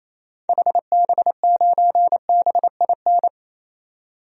Morse code, heard as a keyed tone, about 28 wpm, 700 Hz. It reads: HB9BIN
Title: DXCC-100 Call Signs - 6 Characters - 28wpm